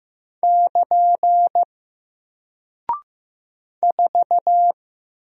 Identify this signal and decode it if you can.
Morse code, heard as a keyed tone, about 15 wpm, 700 Hz. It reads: <KN> 4
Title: Single Character (Letter:Number:Punctuation:Prosign) 15wpm